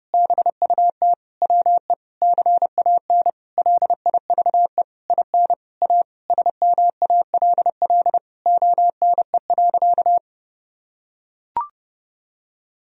Morse code, heard as a keyed tone, about 30 wpm, 700 Hz. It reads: BUT WE CAN LIVE IN A SMALL ONE.
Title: Sentences from Top 1000 Words - Review Part 1 - 30wpm